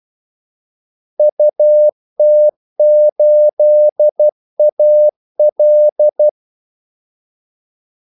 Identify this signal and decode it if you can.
Morse code, heard as a keyed tone, about 12 wpm, 600 Hz. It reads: UT8AL
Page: 2 ekstra ordmellemrum mellem kaldesignalerne